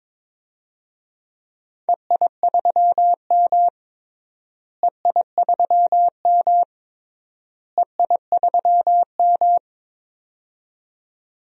Morse code, heard as a keyed tone, about 22 wpm, 700 Hz. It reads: EI3M EI3M EI3M
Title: DXCC-100 Call Signs - 4 Characters - Repeated 3x - 22wpm